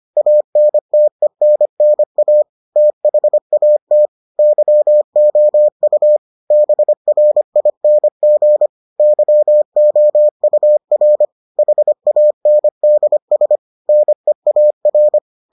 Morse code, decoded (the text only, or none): ANTENNA THAT YOU BRING YOUR HANDS NEAR